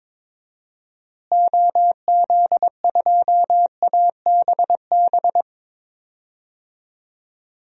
Morse code, heard as a keyed tone, about 22 wpm, 700 Hz. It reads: OZ2ABB
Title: DXCC-100 Call Signs - 6 Characters - Repeated 3x - 22wpm